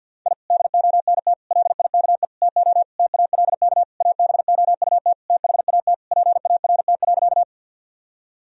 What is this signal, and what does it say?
Morse code, heard as a keyed tone, about 48 wpm, 700 Hz. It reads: I DONT LIKE TO TALK ABOUT THAT PART.
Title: Sentences from Top 1000 Words - Review Part 1 - 50wpm